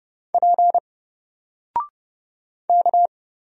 Morse code, heard as a keyed tone, about 31 wpm, 700 Hz. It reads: P K
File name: Single-Letter-Number-30wpm